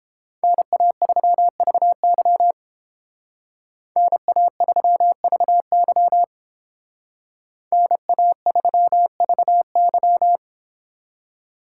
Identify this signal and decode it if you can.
Morse code, heard as a keyed tone, about 29 wpm, 700 Hz. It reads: NA3VY NA3VY NA3VY